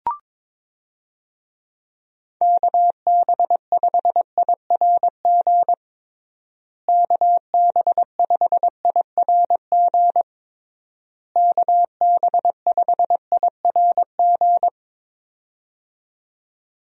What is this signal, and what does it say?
Morse code, heard as a keyed tone, about 22 wpm, 700 Hz. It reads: KB5IRG KB5IRG KB5IRG
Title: USA Call Signs - 6 Characters - Repeated 3x - 22wpm